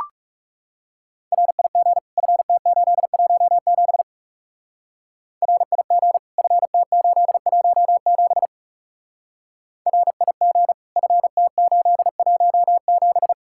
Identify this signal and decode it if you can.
Morse code, heard as a keyed tone, about 36 wpm, 700 Hz. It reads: RIG FT817 RIG FT817 RIG FT817
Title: Speed-Racing Common QSO Phrases - 30wpm